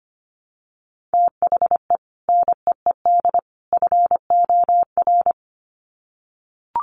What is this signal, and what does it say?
Morse code, heard as a keyed tone, about 25 wpm, 700 Hz. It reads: THE NEED FOR
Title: Sets of 3 Words 25wpm